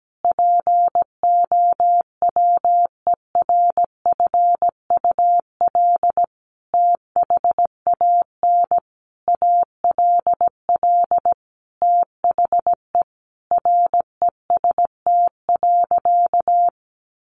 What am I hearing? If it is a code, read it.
Morse code, heard as a keyed tone, about 17 wpm, 700 Hz. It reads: POWERFUL THAN ALL THE REST.
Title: Sentences from Top 1300 Words 17wpm